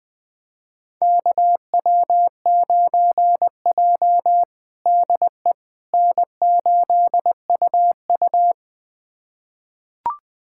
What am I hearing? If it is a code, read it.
Morse code, heard as a keyed tone, about 20 wpm, 700 Hz. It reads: KW9J DE N8UU
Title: USA Callsign DE Callsign - 4 Characters - 20wpm